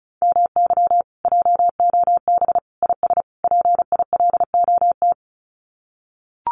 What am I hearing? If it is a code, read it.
Morse code, heard as a keyed tone, about 34 wpm, 700 Hz. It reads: MY JOB IS PILOT
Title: Common QSO Phrases Encore 35wpm